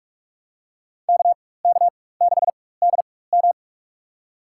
Morse code, heard as a keyed tone, about 48 wpm, spaced as for 21 wpm, 700 Hz. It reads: KK6DM
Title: USA Call Signs - 5 Characters - 4x Character Spacing - 45wpm